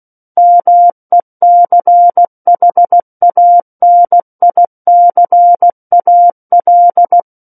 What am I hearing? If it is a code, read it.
Morse code, heard as a keyed tone, about 16 wpm, 700 Hz. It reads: MECHANICAL
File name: Book One / 07-Chapter 7 - 16 WPM